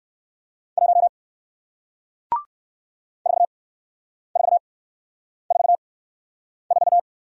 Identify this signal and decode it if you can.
Morse code, heard as a keyed tone, about 57 wpm, spaced as for 9 wpm, 700 Hz. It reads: 1 4444